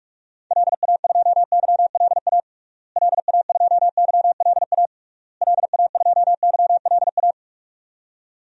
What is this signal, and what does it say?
Morse code, heard as a keyed tone, about 41 wpm, 700 Hz. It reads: LA2YLA LA2YLA LA2YLA
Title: DXCC-100 Call Signs - 6 Characters - Repeated 3x - 45wpm